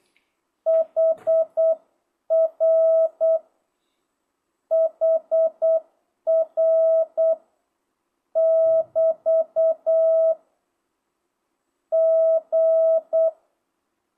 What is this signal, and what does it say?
Morse code, heard as a keyed tone, about 8 wpm, 660 Hz. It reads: HR HR <BT> G
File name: jh8cbh / 2025cw